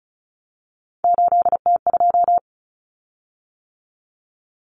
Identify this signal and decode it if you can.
Morse code, heard as a keyed tone, about 34 wpm, 700 Hz. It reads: 8T2
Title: Common Cut Numbers - 3-Digit 35wpm